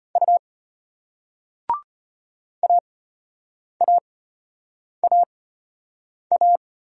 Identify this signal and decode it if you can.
Morse code, heard as a keyed tone, about 34 wpm, spaced as for 8 wpm, 700 Hz. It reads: U AAAA